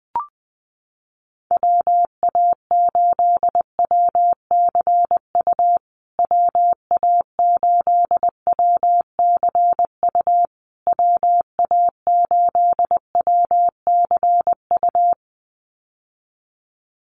Morse code, heard as a keyed tone, about 20 wpm, 700 Hz. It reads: WA8WCU WA8WCU WA8WCU
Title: USA Call Signs Repeated 3x 20wpm